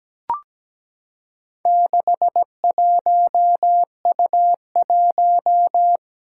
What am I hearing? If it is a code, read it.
Morse code, heard as a keyed tone, about 17 wpm, 700 Hz. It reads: 61U1
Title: Four Challenging Letter-Number Combinations 17wpm